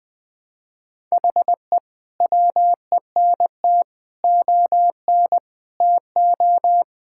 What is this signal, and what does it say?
Morse code, heard as a keyed tone, about 20 wpm, 700 Hz. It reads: HE WENT ON TO
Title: Sentences with 6-Word Phrases - Spoken Beginning - 20wpm